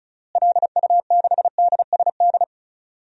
Morse code, heard as a keyed tone, about 34 wpm, 700 Hz. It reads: LU6DSD
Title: DXCC-100 Call Signs - 6 Characters - Repeated 3x - 35wpm